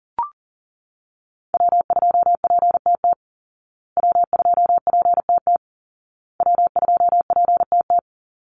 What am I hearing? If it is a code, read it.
Morse code, heard as a keyed tone, about 40 wpm, 700 Hz. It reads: W2PTT W2PTT W2PTT
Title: Call Signs Repeated 2x - 40wpm